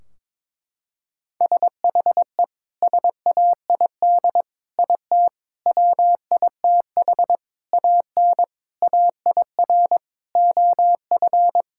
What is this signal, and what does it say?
Morse code, heard as a keyed tone, about 22 wpm, 700 Hz. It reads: SHE SAID IT WITH AN AIR OF
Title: Sentences from Top 1600 Words 22wpm